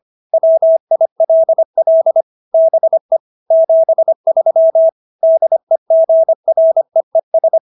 Morse code, heard as a keyed tone, about 25 wpm, 650 Hz. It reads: WILL BE 73 DEGREES